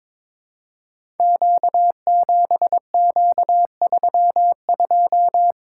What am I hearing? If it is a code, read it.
Morse code, heard as a keyed tone, about 22 wpm, 700 Hz. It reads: Q7Q32
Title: Code Groups - 5 Letter:Numbers - 22wpm